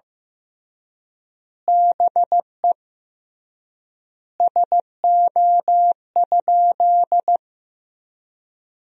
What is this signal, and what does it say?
Morse code, heard as a keyed tone, about 15 wpm, 700 Hz. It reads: BE SO?
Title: 3x Word-Spacing - Sentences from Top 300 Words - 15wpm